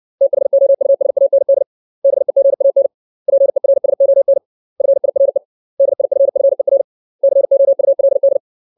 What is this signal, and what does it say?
Morse code, heard as a keyed tone, about 60 wpm, 550 Hz. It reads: THOUSAND BEGAN PERSON FIRE DIFFER COULD
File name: 632 words / 632 Words 60WPM